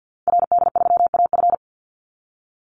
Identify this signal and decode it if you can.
Morse code, heard as a keyed tone, about 48 wpm, 700 Hz. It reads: RD3AF
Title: Call Signs Repeated 2x - 50wpm